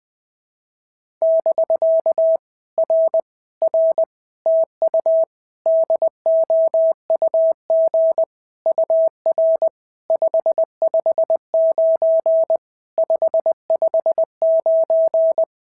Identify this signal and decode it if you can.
Morse code, heard as a keyed tone, about 20 wpm, 650 Hz. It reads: <BK> R R TU DOUG UR 559 559
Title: POTA QSO Practice - 20wpm